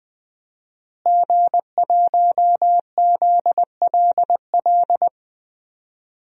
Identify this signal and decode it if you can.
Morse code, heard as a keyed tone, about 21 wpm, 700 Hz. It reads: G1ZLL
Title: DXCC-100 Call Signs - 5 Characters - Repeated 3x - 20wpm